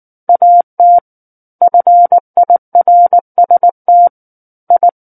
Morse code, heard as a keyed tone, about 19 wpm, 700 Hz. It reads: AT FIRST I